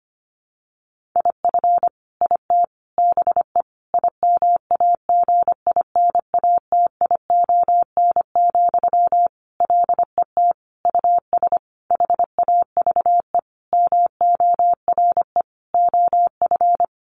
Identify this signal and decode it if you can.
Morse code, heard as a keyed tone, about 25 wpm, 700 Hz. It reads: IF IT BE IMAGINATION, LET US HAVE MORE OF
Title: Sentences from Top 1300 Words 25wpm